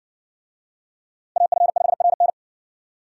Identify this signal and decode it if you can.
Morse code, heard as a keyed tone, about 60 wpm, 700 Hz. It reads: AF5RN